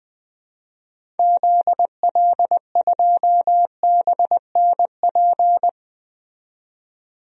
Morse code, heard as a keyed tone, about 20 wpm, 700 Hz. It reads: ZL2BNP
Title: DXCC-100 Call Signs - 6 Characters - 20wpm